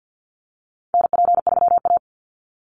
Morse code, heard as a keyed tone, about 50 wpm, 700 Hz. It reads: NP3A
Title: Call Signs Repeated 3x - 50wpm